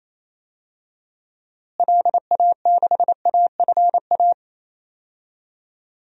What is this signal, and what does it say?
Morse code, heard as a keyed tone, about 28 wpm, 700 Hz. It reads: LA6AFA
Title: DXCC-100 Call Signs - 6 Characters - Repeated 2x - 28wpm